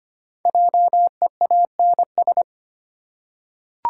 Morse code, heard as a keyed tone, about 25 wpm, 700 Hz. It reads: JEANS
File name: practice / 25wpm / Difficult Words with Hints - 25wpm